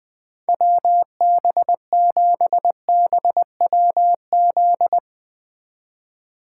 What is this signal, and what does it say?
Morse code, heard as a keyed tone, about 20 wpm, 700 Hz. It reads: WB7BWZ
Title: 2023 SOTA Chaser Call Signs - 20wpm